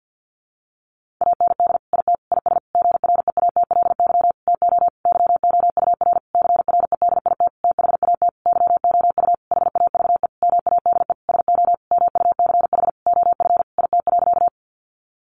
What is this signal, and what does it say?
Morse code, heard as a keyed tone, about 50 wpm, 700 Hz. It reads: AND IT IS GREATLY TO YOUR CREDIT THAT YOU HAVE MADE SO MUCH OF IT.
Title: Sentences from Top 1400 Words 50wpm